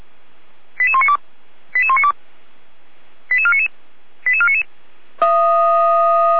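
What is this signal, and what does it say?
Five-tone selective call (ZVEI), heard as quick runs of five tones, tones of about 70 ms. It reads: 88192 88192 88499 88499
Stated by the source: ZVEI five-tone call received by SDR, feuerwehr-orlamuende.org (fire brigade, Orlamünde).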